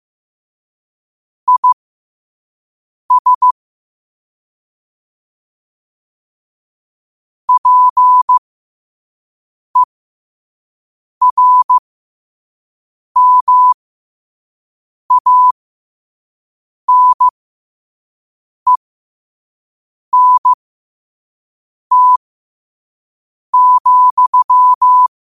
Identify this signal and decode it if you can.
Morse code, heard as a keyed tone, about 15 wpm, spaced as for 5 wpm, 1000 Hz. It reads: IS PERMANENT,
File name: qotdfiles / 5wpm